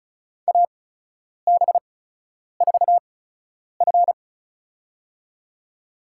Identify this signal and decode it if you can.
Morse code, heard as a keyed tone, about 36 wpm, spaced as for 10 wpm, 700 Hz. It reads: AB4F
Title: USA Call Signs - 8x Character Spacing - 35wpm